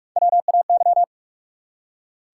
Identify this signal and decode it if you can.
Morse code, heard as a keyed tone, about 46 wpm, 700 Hz. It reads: WAY
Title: Three Letter Words - 45wpm